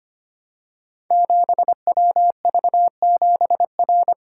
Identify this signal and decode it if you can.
Morse code, heard as a keyed tone, about 25 wpm, 700 Hz. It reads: 7WV7R
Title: Code Groups - 5 Letter:Numbers - 25wpm